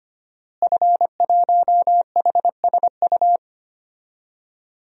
Morse code, heard as a keyed tone, about 25 wpm, 700 Hz. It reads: F1HSU
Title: DXCC-100 Call Signs - 5 Characters - Repeated 2x - 25wpm